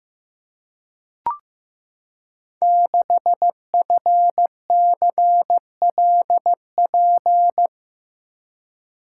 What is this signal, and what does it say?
Morse code, heard as a keyed tone, about 15 wpm, 700 Hz. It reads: 6FCLP
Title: Code Groups - 5 Letter:Numbers - 15wpm